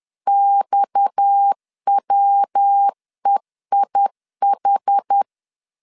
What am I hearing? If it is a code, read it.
Morse code, heard as a keyed tone, about 10 wpm, 800 Hz. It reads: XWEIH
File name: LEZIONI / 6 PKX0 / 6 PKX 10wpm 2426